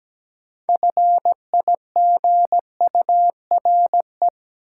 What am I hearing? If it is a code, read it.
Morse code, heard as a keyed tone, about 17 wpm, 700 Hz. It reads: FIGURE
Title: Sets of 4 Words - New Word Drill - 17wpm